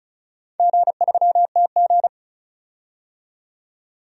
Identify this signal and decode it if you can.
Morse code, heard as a keyed tone, about 36 wpm, 700 Hz. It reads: G3TG
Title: DXCC-100 Call Signs - 4 Characters - 35wpm